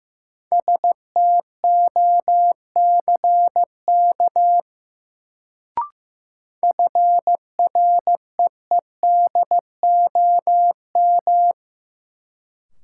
Morse code, heard as a keyed tone, about 15 wpm, 700 Hz. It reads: STOCK FREEDOM
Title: Sentences from Top 800 Words - New Word Drill - 15wpm